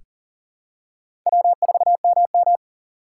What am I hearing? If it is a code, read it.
Morse code, heard as a keyed tone, about 40 wpm, 700 Hz. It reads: W4MM
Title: USA Call Signs - 4 Characters - 3x Character Spacing - 40wpm